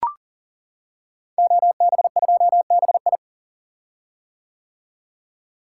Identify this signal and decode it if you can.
Morse code, heard as a keyed tone, about 40 wpm, 700 Hz. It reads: OB2BI
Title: Code Groups - 5 Letter:Numbers - 40wpm